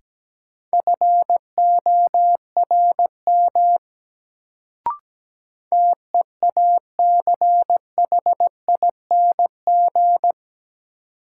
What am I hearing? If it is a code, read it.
Morse code, heard as a keyed tone, about 17 wpm, 700 Hz. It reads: FORM TEACHING
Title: Sentences from Top 1100 Words - New Word Drill Alt - 17wpm